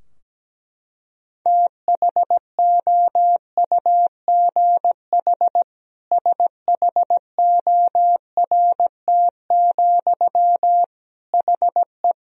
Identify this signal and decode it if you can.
Morse code, heard as a keyed tone, about 17 wpm, 700 Hz. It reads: THOUGH SHORT, HE
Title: Sentences from Top 1000 Words - Review Part 3 - 17wpm